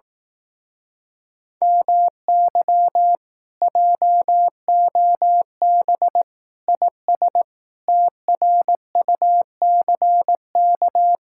Morse code, heard as a keyed tone, about 18 wpm, 700 Hz. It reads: MY JOB IS TRUCK
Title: Speed-Racing Common QSO Phrases Encore - 15wpm